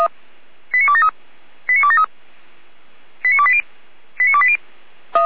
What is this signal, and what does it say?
Five-tone selective call (ZVEI), heard as quick runs of five tones, tones of about 70 ms. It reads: 89273 89273 89280 89280